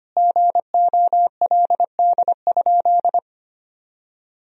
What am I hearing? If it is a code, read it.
Morse code, heard as a keyed tone, about 25 wpm, 700 Hz. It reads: GOLD?
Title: Sentences from Top 1000 Words - Review Part 3 - 25wpm